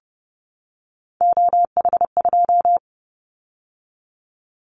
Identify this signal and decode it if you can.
Morse code, heard as a keyed tone, about 30 wpm, 700 Hz. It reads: OH2
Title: Callsign Prefixes for Top 50 DXCC - 30wpm